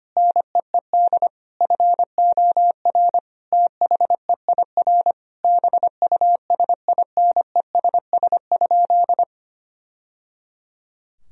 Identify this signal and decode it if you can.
Morse code, heard as a keyed tone, about 25 wpm, 700 Hz. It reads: NEED FOR THEIR BUSINESS?
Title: Sentences from Top 1000 Words - Review Part 4 - 25wpm